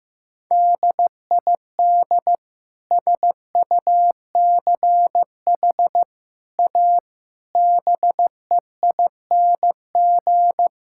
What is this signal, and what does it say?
Morse code, heard as a keyed tone, about 15 wpm, 700 Hz. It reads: DID SUCH A BEING
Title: Sentences from Top 1000 Words - Review Part 5 - 15wpm